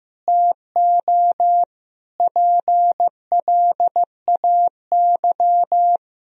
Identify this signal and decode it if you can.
Morse code, heard as a keyed tone, about 16 wpm, 700 Hz. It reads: TO PLAY
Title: Sentences from Top 1000 Words - Review Part 3 - 15wpm